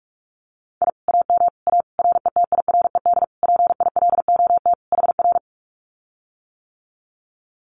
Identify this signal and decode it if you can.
Morse code, heard as a keyed tone, about 44 wpm, 700 Hz. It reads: I AM A RETIRED PILOT HR
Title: Common QSO Phrases Encore 45wpm